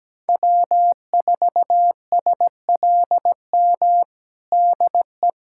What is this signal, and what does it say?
Morse code, heard as a keyed tone, about 17 wpm, 700 Hz. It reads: W4SLM DE
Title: USA Callsign DE Callsign - 5 Characters - 17wpm